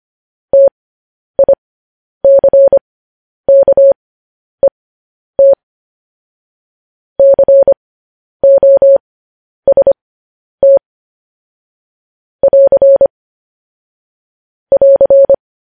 Morse code, heard as a keyed tone, about 25 wpm, spaced as for 10 wpm, 550 Hz. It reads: TICKET COST <AR> <AR>